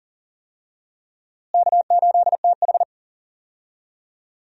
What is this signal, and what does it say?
Morse code, heard as a keyed tone, about 40 wpm, 700 Hz. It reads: K8TH
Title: USA Call Signs - 4 Characters - Repeated 3x - 40wpm